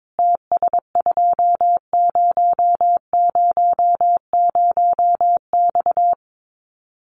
Morse code, heard as a keyed tone, about 22 wpm, 700 Hz. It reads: TS2000X
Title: Common QSO Phrases 22wpm